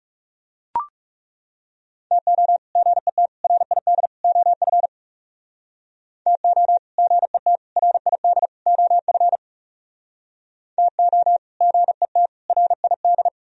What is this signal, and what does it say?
Morse code, heard as a keyed tone, about 40 wpm, 700 Hz. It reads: TO GET RID OF TO GET RID OF TO GET RID